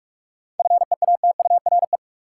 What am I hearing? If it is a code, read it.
Morse code, heard as a keyed tone, about 46 wpm, 700 Hz. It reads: FEATURE
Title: Seven Letter Words - 45wpm